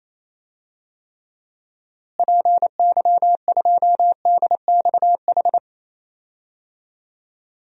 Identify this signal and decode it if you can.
Morse code, heard as a keyed tone, about 28 wpm, 700 Hz. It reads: PY2DXH
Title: DXCC-100 Call Signs - 6 Characters - Repeated 3x - 28wpm